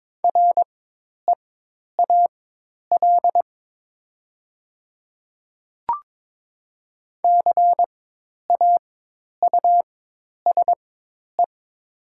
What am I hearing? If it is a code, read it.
Morse code, heard as a keyed tone, about 22 wpm, spaced as for 10 wpm, 700 Hz. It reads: REAL CAUSE
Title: 4x Character Spacing - Top 300 Words - 22wpm